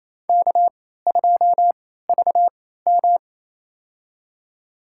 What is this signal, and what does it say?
Morse code, heard as a keyed tone, about 28 wpm, spaced as for 16 wpm, 700 Hz. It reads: K2VM
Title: USA Call Signs - 3x Character Spacing - 28wpm